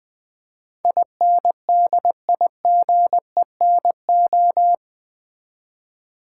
Morse code, heard as a keyed tone, about 20 wpm, 700 Hz. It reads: INDIGENO
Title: Build A Word - 4 Letters At A Time - 10-Letter Words - 20wpm